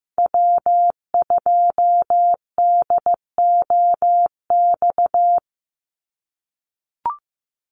Morse code, heard as a keyed tone, about 15 wpm, 700 Hz. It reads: W2DOX
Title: USA Call Signs 15wpm